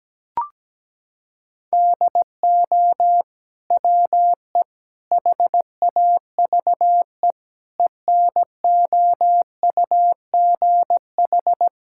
Morse code, heard as a keyed tone, about 17 wpm, 700 Hz. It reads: DO WE HAVE ENOUGH
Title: Sentences from Top 1000 Words - Review Part 5 - 17wpm